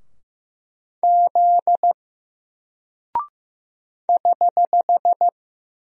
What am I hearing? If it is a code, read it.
Morse code, heard as a keyed tone, about 15 wpm, 700 Hz. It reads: Z <HH>
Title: Single Character (Letter:Number:Punctuation:Prosign) 15wpm